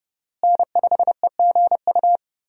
Morse code, heard as a keyed tone, about 30 wpm, 700 Hz. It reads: N5EGU
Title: USA Call Signs - 5 Characters - Repeated 3x - 30wpm